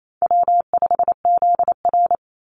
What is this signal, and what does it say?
Morse code, heard as a keyed tone, about 28 wpm, 700 Hz. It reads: W5ZR
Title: Easy Call Signs Repeated 2x - 28wpm